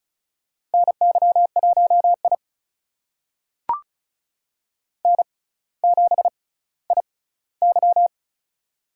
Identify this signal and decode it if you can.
Morse code, heard as a keyed tone, about 36 wpm, spaced as for 18 wpm, 700 Hz. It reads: NY1I N7IY
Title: USA Call Signs - 4 Characters - 6x Character Spacing - 35wpm